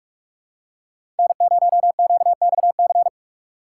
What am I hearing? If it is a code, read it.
Morse code, heard as a keyed tone, about 41 wpm, 700 Hz. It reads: N0QXC
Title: USA Call Signs - 5 Characters - 4x Character Spacing - 45wpm